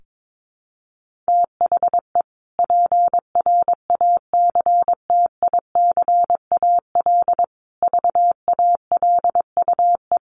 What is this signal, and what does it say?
Morse code, heard as a keyed tone, about 22 wpm, 700 Hz. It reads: THE PRACTICAL VALUE
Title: Sentences from Top 1400 Words 22wpm